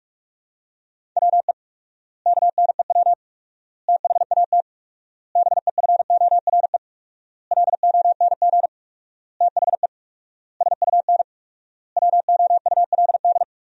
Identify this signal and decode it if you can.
Morse code, heard as a keyed tone, about 42 wpm, 700 Hz. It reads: WE KNEW THAT BEFORE LONG THE SUN WOULD
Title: 4x Word-Spacing - Sentences from Top 400 Words - 45wpm